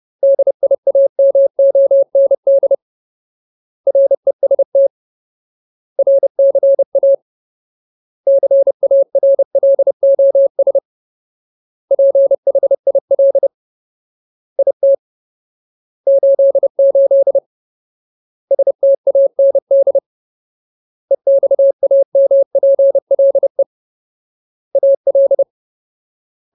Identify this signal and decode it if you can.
Morse code, heard as a keyed tone, about 30 wpm, 550 Hz. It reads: DIAMOND REST RCA CARLOS PHIL IT 88 STAND EXAMPLE AL
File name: Ultimate Word List / 30WPM Ultimate